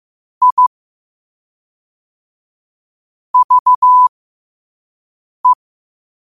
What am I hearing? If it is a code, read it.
Morse code, heard as a keyed tone, about 15 wpm, spaced as for 5 wpm, 1000 Hz. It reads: I VE
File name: 5wpm